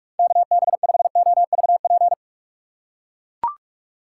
Morse code, heard as a keyed tone, about 44 wpm, 700 Hz. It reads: KB5OVP